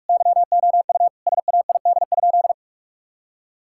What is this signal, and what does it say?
Morse code, heard as a keyed tone, about 46 wpm, 700 Hz. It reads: YOU SAID?
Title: Sentences from Top 1600 Words 45wpm